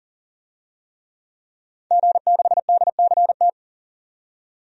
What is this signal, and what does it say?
Morse code, heard as a keyed tone, about 38 wpm, 700 Hz. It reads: G6DCT
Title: DXCC-100 Call Signs - 5 Characters - 40wpm